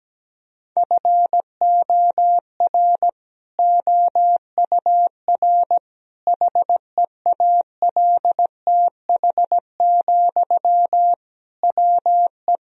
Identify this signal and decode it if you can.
Morse code, heard as a keyed tone, about 17 wpm, 700 Hz. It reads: FOR OUR HEALTH, WE
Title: Sentences from Top 1000 Words - Review Part 4 - 17wpm